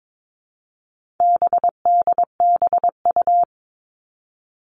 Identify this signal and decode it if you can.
Morse code, heard as a keyed tone, about 22 wpm, 700 Hz. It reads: BDBU
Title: All Cut Numbers - 4-Digit 22wpm